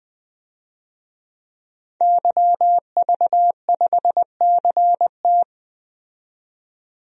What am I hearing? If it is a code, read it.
Morse code, heard as a keyed tone, about 20 wpm, 700 Hz. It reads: YV5CT
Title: DXCC-100 Call Signs - 5 Characters - 20wpm